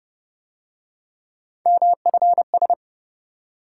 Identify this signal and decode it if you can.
Morse code, heard as a keyed tone, about 33 wpm, 700 Hz. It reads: MFS